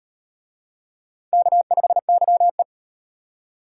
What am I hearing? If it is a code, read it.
Morse code, heard as a keyed tone, about 35 wpm, 700 Hz. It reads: K5YE